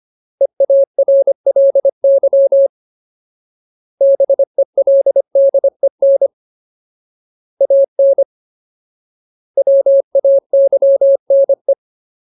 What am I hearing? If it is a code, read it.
Morse code, heard as a keyed tone, about 25 wpm, 550 Hz. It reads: EARLY BELDEN AN WAYNE
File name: Ultimate Word List / 25WPM Ultimate